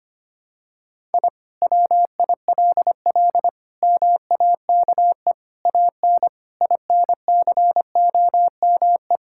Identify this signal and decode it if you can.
Morse code, heard as a keyed tone, about 25 wpm, 700 Hz. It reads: I WILL MAKE AN INCOME
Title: Sentences from Top 1000 Words - Review Part 5 - 25wpm